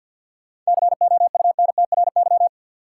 Morse code, heard as a keyed tone, about 52 wpm, 700 Hz. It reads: COUNTRY
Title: Sentences from Top 300 Words - New Word Drill - 50wpm